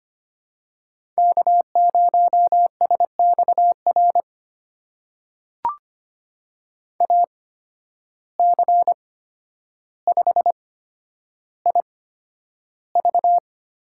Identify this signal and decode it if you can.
Morse code, heard as a keyed tone, about 25 wpm, spaced as for 11 wpm, 700 Hz. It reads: K0SXR AC5IV
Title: USA Call Signs - 5 Characters - 8x Character Spacing - 25wpm